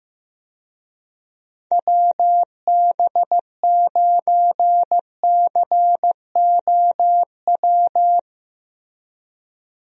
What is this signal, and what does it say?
Morse code, heard as a keyed tone, about 15 wpm, 700 Hz. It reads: WB9COW
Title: USA Call Signs - 6 Characters - Repeated 2x - 15wpm